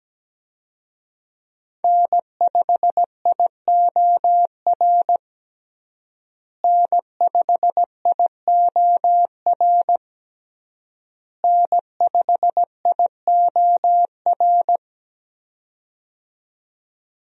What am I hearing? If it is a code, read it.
Morse code, heard as a keyed tone, about 17 wpm, 700 Hz. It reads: N5IOR N5IOR N5IOR
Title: USA Call Signs - 5 Characters - Repeated 3x - 17wpm